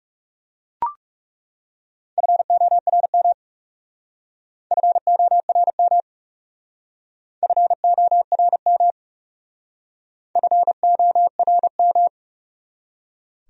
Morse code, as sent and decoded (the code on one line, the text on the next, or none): ..-. --- .-. -- / ..-. --- .-. -- / ..-. --- .-. -- / ..-. --- .-. --
FORM FORM FORM FORM